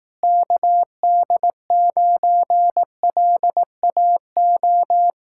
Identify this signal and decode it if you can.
Morse code, heard as a keyed tone, about 18 wpm, 700 Hz. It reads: KD9LAO
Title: Speed-Racing Common QSO Phrases Encore - 15wpm